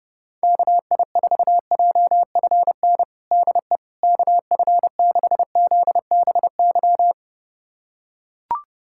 Morse code, heard as a keyed tone, about 30 wpm, 700 Hz. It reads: KI4JFN DE KF6ZBY
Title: USA Callsign DE Callsign - 6 Characters - 30wpm